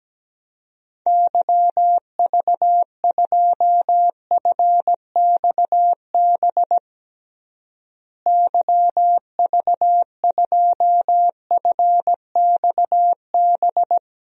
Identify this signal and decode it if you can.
Morse code, heard as a keyed tone, about 17 wpm, 700 Hz. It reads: YV2FXB YV2FXB
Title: DXCC-100 Call Signs - 6 Characters - Repeated 3x - 17wpm